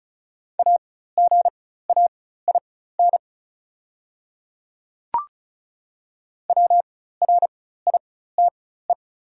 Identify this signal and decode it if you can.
Morse code, heard as a keyed tone, about 35 wpm, spaced as for 16 wpm, 690 Hz. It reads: AGAIN WRITE
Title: 4x Character Spacing - Top 500 Words - 35wpm